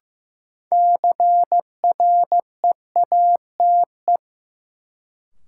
Morse code, heard as a keyed tone, about 15 wpm, 700 Hz. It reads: CREATE
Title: Sentences from Top 1600 Words - New Word Drill Alt - 15wpm